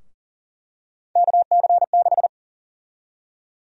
Morse code, heard as a keyed tone, about 40 wpm, 700 Hz. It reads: KC6
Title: USA Call Sign Prefixes - 40wpm